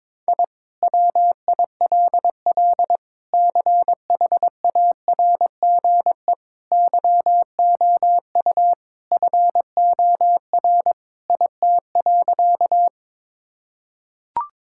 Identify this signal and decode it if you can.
Morse code, heard as a keyed tone, about 22 wpm, 700 Hz. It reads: I WILL CHARGE YOU FOR IT.
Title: Sentences from Top 1000 Words - Review Part 6 - 22wpm